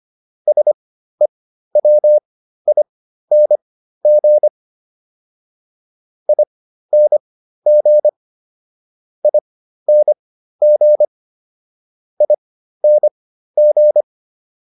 Morse code, heard as a keyed tone, about 25 wpm, spaced as for 13 wpm, 600 Hz. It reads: SEWING ING ING ING